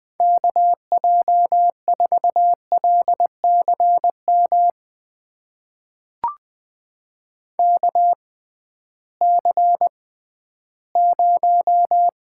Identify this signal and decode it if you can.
Morse code, heard as a keyed tone, about 20 wpm, spaced as for 13 wpm, 700 Hz. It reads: KJ4LCM KC0
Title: USA Call Signs - 6 Characters - 6x Character Spacing - 20wpm